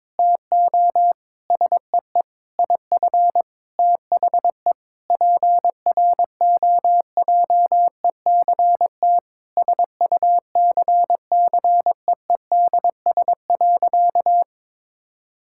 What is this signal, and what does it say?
Morse code, heard as a keyed tone, about 22 wpm, 700 Hz. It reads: TO SEE IF THE PROJECT SUCCEEDS.
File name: practice / 22wpm / Sentences with 8-Word Phrases - 22wpm